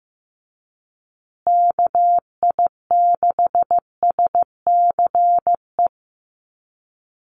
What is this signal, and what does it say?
Morse code, heard as a keyed tone, about 14 wpm, 700 Hz. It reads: KI6SCE
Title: USA Call Signs Repeated 2x 15wpm